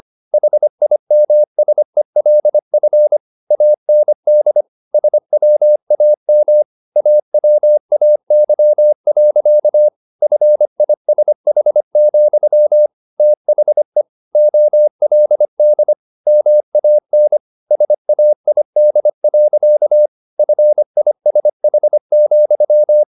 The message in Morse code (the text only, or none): HIMSELF AND SWAM AWAY. FISH, THE OLD MAN SAID. FISH,